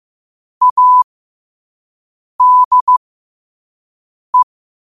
Morse code, heard as a keyed tone, about 15 wpm, spaced as for 5 wpm, 1000 Hz. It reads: ADE
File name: qotdfiles / 5wpm